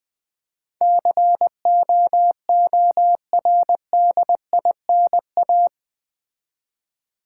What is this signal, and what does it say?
Morse code, heard as a keyed tone, about 20 wpm, 700 Hz. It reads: COORDINA
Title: Build A Word - 4 Letters At A Time - 11-Letter Words - 20wpm